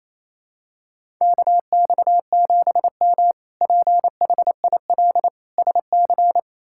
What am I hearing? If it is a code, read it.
Morse code, heard as a keyed tone, about 28 wpm, 700 Hz. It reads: KX7M PHIL SC
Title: K1USN SST QSO Practice - 28wpm